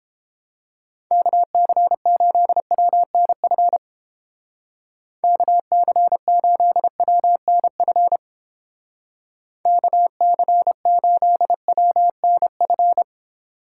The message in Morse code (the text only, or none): KC8WNF KC8WNF KC8WNF